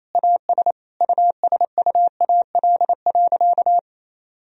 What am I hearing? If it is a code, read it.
Morse code, heard as a keyed tone, about 27 wpm, 700 Hz. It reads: AS USUAL.
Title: Sentences from Top 1000 Words - Review Part 2 - 28wpm